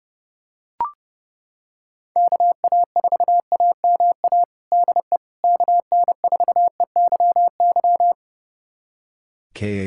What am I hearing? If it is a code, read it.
Morse code, heard as a keyed tone, about 32 wpm, 700 Hz. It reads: KA4AMA DE KN4EYY
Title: USA Callsign DE Callsign 30wpm